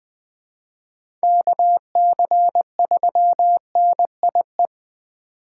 Morse code, heard as a keyed tone, about 20 wpm, 700 Hz. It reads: KC3NIE
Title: K1USN SST QSO Practice - 20wpm